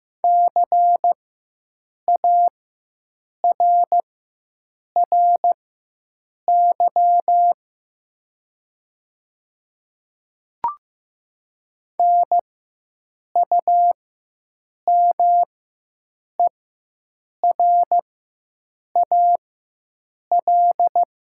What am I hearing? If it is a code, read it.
Morse code, heard as a keyed tone, about 15 wpm, spaced as for 7 wpm, 700 Hz. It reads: CARRY NUMERAL
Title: 4x Character Spacing - Top 500 Words - 15wpm